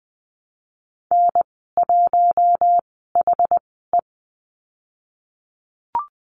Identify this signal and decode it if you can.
Morse code, heard as a keyed tone, about 20 wpm, spaced as for 14 wpm, 700 Hz. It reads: N1HE
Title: 088 Lesson 15g - Number 5 - Call Signs 20wpm